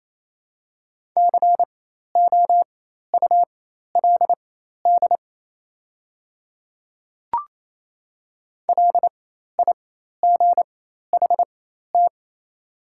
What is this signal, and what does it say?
Morse code, heard as a keyed tone, about 28 wpm, spaced as for 13 wpm, 690 Hz. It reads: COULD LIGHT
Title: 4x Character Spacing - Top 200 Words - 28wpm